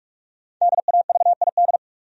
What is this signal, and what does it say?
Morse code, heard as a keyed tone, about 46 wpm, 700 Hz. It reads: DAVID
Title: USA Common Names - 45wpm